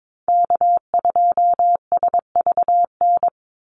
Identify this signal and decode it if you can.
Morse code, heard as a keyed tone, about 22 wpm, 700 Hz. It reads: K2SVN
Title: USA Call Signs 22wpm